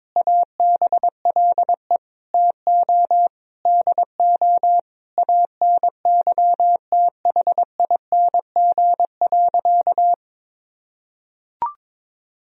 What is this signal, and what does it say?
Morse code, heard as a keyed tone, about 22 wpm, 700 Hz. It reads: ABLE TO DO ANYTHING.
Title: Sentences from Top 1000 Words - Review Part 2 - 22wpm